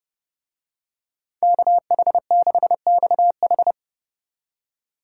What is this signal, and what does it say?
Morse code, heard as a keyed tone, about 28 wpm, 700 Hz. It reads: KH6XH